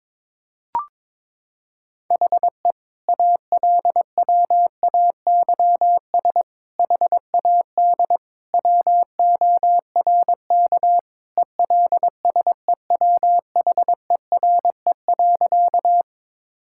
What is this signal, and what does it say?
Morse code, heard as a keyed tone, about 22 wpm, 700 Hz. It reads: HE ALWAYS HAD WORK ELSEWHERE.
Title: Sentences from Top 1600 Words 22wpm